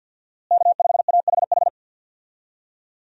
Morse code, heard as a keyed tone, about 52 wpm, 700 Hz. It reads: K5AHH
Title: USA Call Signs - 5 Characters - Repeated 3x - 50wpm